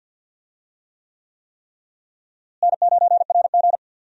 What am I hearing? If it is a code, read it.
Morse code, heard as a keyed tone, about 50 wpm, 700 Hz. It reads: N9RG
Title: USA Call Signs - 4 Characters - Repeated 3x - 50wpm